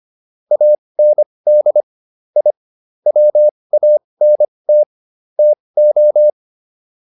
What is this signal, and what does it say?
Morse code, heard as a keyed tone, about 25 wpm, spaced as for 20 wpm, 600 Hz. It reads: AND I WANT TO